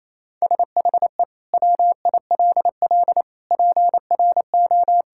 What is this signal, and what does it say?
Morse code, heard as a keyed tone, about 28 wpm, 700 Hz. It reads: SHE WILL PRO
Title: Sentences from Top 1000 Words - Review Part 6 - 28wpm